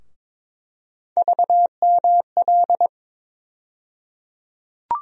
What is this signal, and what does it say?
Morse code, heard as a keyed tone, about 23 wpm, 700 Hz. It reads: VML
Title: USA Call Sign Postfixes - 3 Letters 22wpm